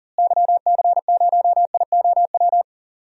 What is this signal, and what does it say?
Morse code, heard as a keyed tone, about 40 wpm, 700 Hz. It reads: YC0IOW
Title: DXCC-100 Call Signs - 6 Characters - Repeated 3x - 40wpm